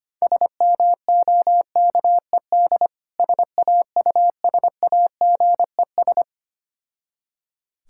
Morse code, heard as a keyed tone, about 25 wpm, 690 Hz. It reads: SMOKED SAUSAGES
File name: practice / 25wpm / Difficult Words with Hints - 25wpm